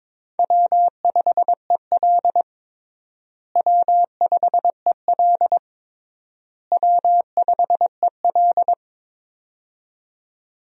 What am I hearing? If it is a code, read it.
Morse code, heard as a keyed tone, about 22 wpm, 700 Hz. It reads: W5EL W5EL W5EL
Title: USA Call Signs - 4 Characters - Repeated 3x - 22wpm